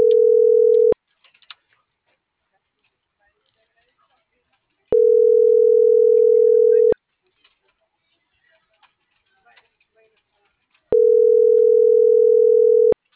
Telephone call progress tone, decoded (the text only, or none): ringback tone